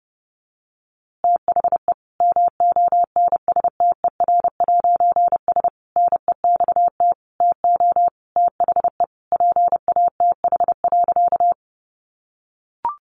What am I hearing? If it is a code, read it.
Morse code, heard as a keyed tone, about 30 wpm, 700 Hz. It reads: THE MONSTER'S NEXT TO THE PATH.
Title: A Sound Of Thunder 30wpm